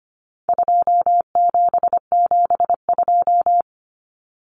Morse code, heard as a keyed tone, about 25 wpm, 700 Hz. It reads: 2772
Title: Common Cut Numbers - 4-Digit 25wpm